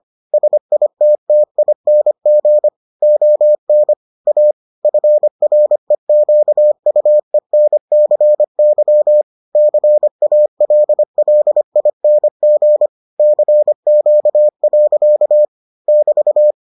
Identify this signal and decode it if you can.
Morse code, heard as a keyed tone, about 25 wpm, 600 Hz. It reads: SITTING ON A FREQUENCY CALLING CQ. <BT>